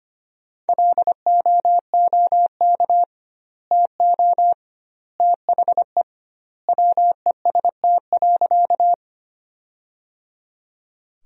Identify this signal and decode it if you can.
Morse code, heard as a keyed tone, about 25 wpm, 700 Hz. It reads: LOOK TO THE WEST.
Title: 2x Word-Spacing - Sentences from Top 400 Words - 25wpm